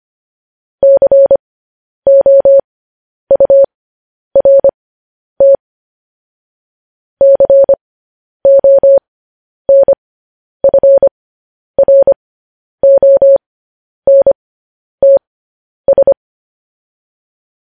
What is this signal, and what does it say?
Morse code, heard as a keyed tone, about 25 wpm, spaced as for 10 wpm, 550 Hz. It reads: COURT CONFRONTS
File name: podcast / News headlines-1x-25x10